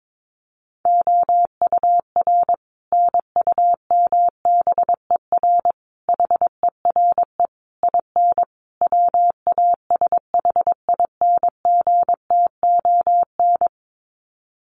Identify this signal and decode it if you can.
Morse code, heard as a keyed tone, about 22 wpm, 700 Hz. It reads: OUR NUMBER HERE IN WASHINGTON
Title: Sets of 5 Words Encore2 22wpm